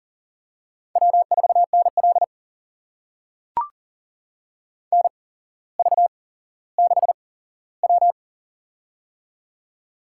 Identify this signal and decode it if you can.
Morse code, heard as a keyed tone, about 39 wpm, spaced as for 17 wpm, 700 Hz. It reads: W4NL NV6W
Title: USA Call Signs - 8x Character Spacing - 40wpm